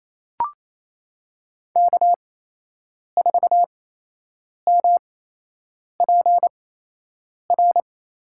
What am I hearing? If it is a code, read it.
Morse code, heard as a keyed tone, about 27 wpm, spaced as for 8 wpm, 700 Hz. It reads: K4MPR
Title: USA Call Signs - 5 Characters - 8x Character Spacing - 28wpm